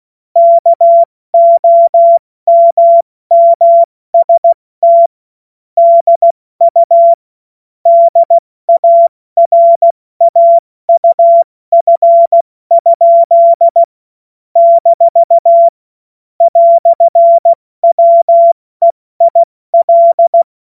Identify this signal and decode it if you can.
Morse code, heard as a keyed tone, about 16 wpm, 680 Hz. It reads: KOMMST DU DARAUF? - "WEIL
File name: Witz-22-Juli-2025-70-F80